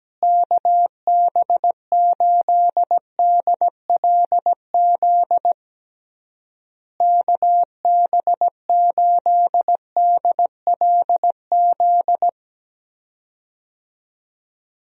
Morse code, heard as a keyed tone, about 17 wpm, 700 Hz. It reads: KB8DLZ KB8DLZ
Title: USA Call Signs - 6 Characters - Repeated 3x - 17wpm